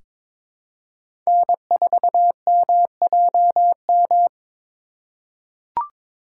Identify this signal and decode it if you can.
Morse code, heard as a keyed tone, about 22 wpm, 700 Hz. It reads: N4MJM
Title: USA Call Signs - 2x Character Spacing - 22wpm